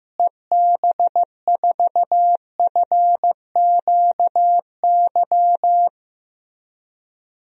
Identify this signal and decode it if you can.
Morse code, heard as a keyed tone, about 15 wpm, 700 Hz. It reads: EB4FQY
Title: DXCC-100 Call Signs - 6 Characters - Repeated 3x - 15wpm